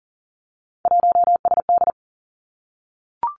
Morse code, heard as a keyed tone, about 40 wpm, 700 Hz. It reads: 1SD
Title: Three Challenging Letter-Number Combinations 40wpm